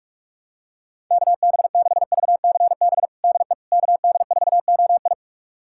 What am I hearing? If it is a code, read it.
Morse code, heard as a keyed tone, about 46 wpm, 700 Hz. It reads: KB6VCB DE KD4OI
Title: USA Callsign DE Callsign 45wpm